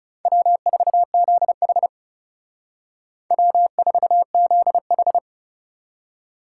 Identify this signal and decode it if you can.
Morse code, heard as a keyed tone, about 36 wpm, 700 Hz. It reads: W4ZH W4ZH
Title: Speed Racing - USA 1x2 Call Signs 30wpm